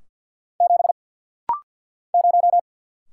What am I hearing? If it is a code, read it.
Morse code, heard as a keyed tone, about 51 wpm, 690 Hz. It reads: 7 0
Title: Single Letter-Number - Rapid-Fire 50wpm